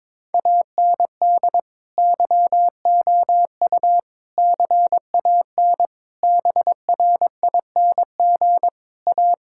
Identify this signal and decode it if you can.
Morse code, heard as a keyed tone, about 21 wpm, 700 Hz. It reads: AND YOU CAN BRING A
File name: practice / 22wpm / Sentences with 2-Word Phrases - 22wpm